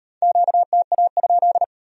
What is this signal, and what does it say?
Morse code, heard as a keyed tone, about 37 wpm, 700 Hz. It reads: QTA?